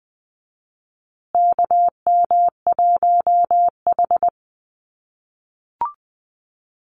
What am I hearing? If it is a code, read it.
Morse code, heard as a keyed tone, about 20 wpm, 700 Hz. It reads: KM1H
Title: USA Call Signs Repeated 3x 20wpm